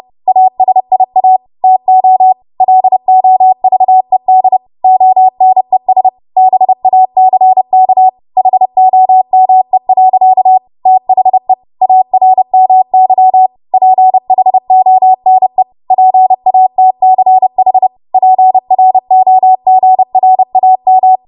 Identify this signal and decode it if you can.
Morse code, heard as a keyed tone, about 30 wpm, 750 Hz. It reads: ASIA TO LOVED ONES BACK HOME. THE ARMY PHONE PATCH PROGRAM
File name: Archive / 30 WPM / 30WPM